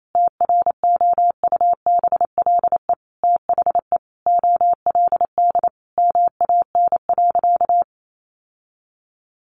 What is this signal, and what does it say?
Morse code, heard as a keyed tone, about 28 wpm, 700 Hz. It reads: TROUBLE THE OLD MAN.
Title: Sentences from Top 1200 Words 28wpm